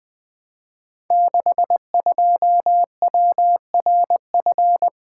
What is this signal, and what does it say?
Morse code, heard as a keyed tone, about 20 wpm, 700 Hz. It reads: 62WRF
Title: Code Groups - 5 Letter:Numbers - 20wpm